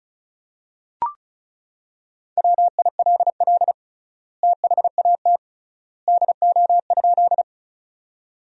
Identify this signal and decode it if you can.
Morse code, heard as a keyed tone, about 36 wpm, 700 Hz. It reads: WILL THAT DO?
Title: 3x Word-Spacing - Sentences from Top 100 Words - 35wpm